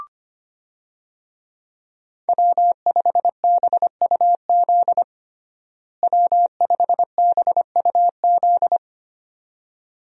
Morse code, heard as a keyed tone, about 25 wpm, 700 Hz. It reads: W5BUZ W5BUZ
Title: USA Call Signs - 5 Characters - Repeated 2x - 25wpm